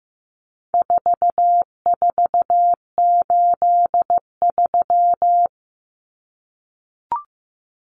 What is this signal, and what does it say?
Morse code, heard as a keyed tone, about 15 wpm, 700 Hz. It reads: 4483
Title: Common Cut Numbers - 4-Digit 15wpm